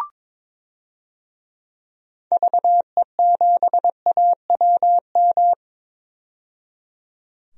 Morse code, heard as a keyed tone, about 22 wpm, 700 Hz. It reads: VE7AWM
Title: DXCC-100 Call Signs - 6 Characters - 22wpm